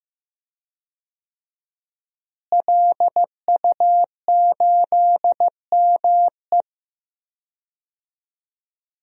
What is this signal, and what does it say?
Morse code, heard as a keyed tone, about 15 wpm, 700 Hz. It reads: LU8ME
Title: DXCC-100 Call Signs - 5 Characters - Repeated 2x - 15wpm